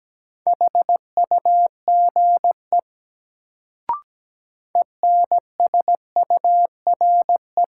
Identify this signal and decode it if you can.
Morse code, heard as a keyed tone, about 17 wpm, 700 Hz. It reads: HUGE ENSURE
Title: Sentences from Top 1100 Words - New Word Drill - 17wpm